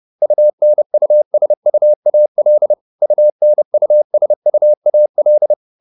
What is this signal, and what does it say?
Morse code, heard as a keyed tone, about 30 wpm, 600 Hz. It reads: UNUSUAL UNUSUAL